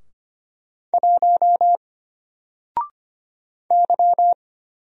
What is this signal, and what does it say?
Morse code, heard as a keyed tone, about 25 wpm, 700 Hz. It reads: 1 Y
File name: Single-Letter-Number-25wpm